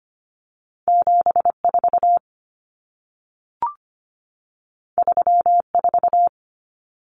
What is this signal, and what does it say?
Morse code, heard as a keyed tone, about 25 wpm, 700 Hz. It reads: 74 34
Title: Common Cut Numbers - 2-Digit 25wpm